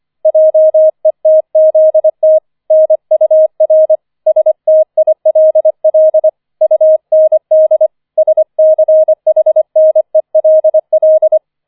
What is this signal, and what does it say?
Morse code, heard as a keyed tone, about 25 wpm, 630 Hz. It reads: JETZT NUR STILL UND SCHNELL
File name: 004 Vierter Streich 25wpm